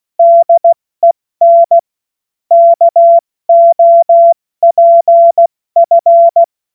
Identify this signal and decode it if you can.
Morse code, heard as a keyed tone, about 16 wpm, 680 Hz. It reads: DEN KOPF
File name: Morsekurs / Witze / Witz 14.November 2023 70 F80